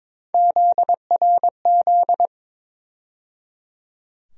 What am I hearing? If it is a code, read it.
Morse code, heard as a keyed tone, about 22 wpm, 700 Hz. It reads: ZRZ
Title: USA Call Sign Postfixes - 3 Letters 22wpm